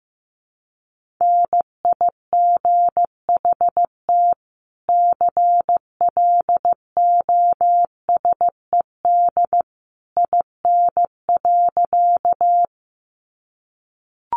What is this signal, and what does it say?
Morse code, heard as a keyed tone, about 15 wpm, 700 Hz. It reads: NIGHT CLOSED IN.
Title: Sentences from Top 1400 Words 15wpm